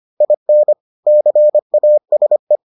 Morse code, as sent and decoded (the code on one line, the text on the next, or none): .. -. / -.-. .- ... .
IN CASE